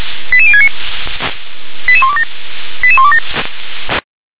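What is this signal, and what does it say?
Five-tone selective call (ZVEI), heard as quick runs of five tones, tones of about 70 ms. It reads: 88069 88127 88127
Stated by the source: SDR-received ZVEI five-tone call 88040.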